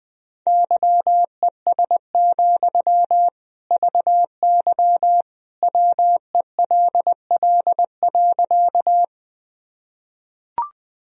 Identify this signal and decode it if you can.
Morse code, heard as a keyed tone, about 21 wpm, 700 Hz. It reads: YES, VY WELL.
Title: Sentences from Top 500 Words and Abbreviations 20wpm